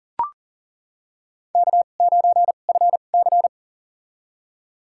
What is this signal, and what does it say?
Morse code, heard as a keyed tone, about 40 wpm, spaced as for 29 wpm, 700 Hz. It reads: K9FC
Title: USA Call Signs - 2x Character Spacing - 40wpm